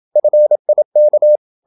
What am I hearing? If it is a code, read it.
Morse code, heard as a keyed tone, about 27 wpm, 600 Hz. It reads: FIK
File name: Ferie QRQ 27wpm